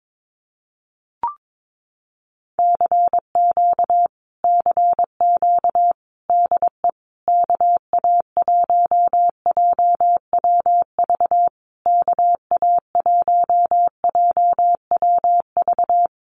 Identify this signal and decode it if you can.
Morse code, heard as a keyed tone, about 22 wpm, 700 Hz. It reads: CQ CQ DE KA1JWV KA1JWV
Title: Common QSO Phrases Encore 22wpm